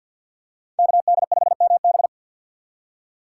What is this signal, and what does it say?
Morse code, heard as a keyed tone, about 52 wpm, 700 Hz. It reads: KD5MB
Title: USA Call Signs - 4x Character Spacing - 50wpm